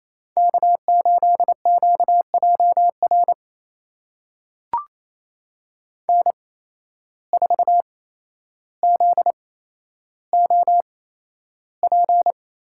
Morse code, heard as a keyed tone, about 28 wpm, spaced as for 12 wpm, 700 Hz. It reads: K8QJR N4ZOP
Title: USA Call Signs - 8x Character Spacing - 28wpm